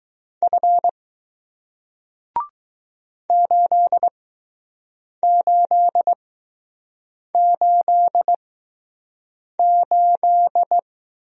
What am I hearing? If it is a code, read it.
Morse code, heard as a keyed tone, about 20 wpm, spaced as for 6 wpm, 700 Hz. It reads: F 8888